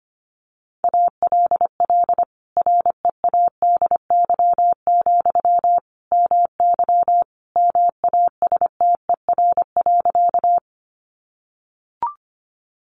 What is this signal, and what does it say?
Morse code, heard as a keyed tone, about 25 wpm, 700 Hz. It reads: ALL READY, MY MASTER.
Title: Sentences from Top 1300 Words 25wpm